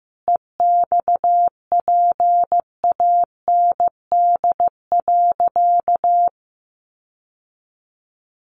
Morse code, heard as a keyed tone, about 15 wpm, 700 Hz. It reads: EXPAND.
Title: Sentences from Top 1400 Words 15wpm